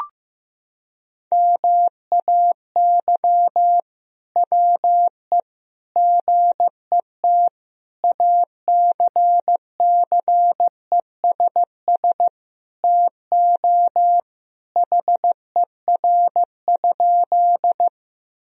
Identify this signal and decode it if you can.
Morse code, heard as a keyed tone, about 15 wpm, 700 Hz. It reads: MAY WE GET ACCESS TO HER?
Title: Sentences from Top 1000 Words - Review Part 4 - 15wpm